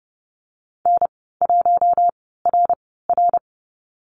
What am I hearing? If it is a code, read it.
Morse code, heard as a keyed tone, about 30 wpm, spaced as for 17 wpm, 700 Hz. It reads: N1RR
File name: Lesson 14g - Number 2 - Call Signs 30wpm